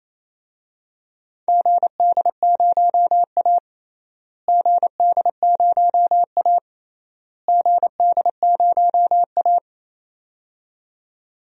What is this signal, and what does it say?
Morse code, heard as a keyed tone, about 28 wpm, 700 Hz. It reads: GD0A GD0A GD0A
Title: DXCC-100 Call Signs - 4 Characters - Repeated 3x - 28wpm